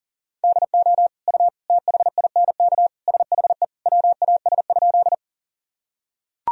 Morse code, heard as a keyed tone, about 40 wpm, 700 Hz. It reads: DO U THINK SHE WAS?
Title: Sentences from Top 200 Words and Abbreviations 40wpm